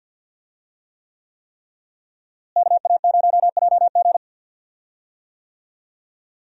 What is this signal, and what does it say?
Morse code, heard as a keyed tone, about 49 wpm, 700 Hz. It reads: KA0JG
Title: USA Call Signs - 5 Characters - Repeated 3x - 50wpm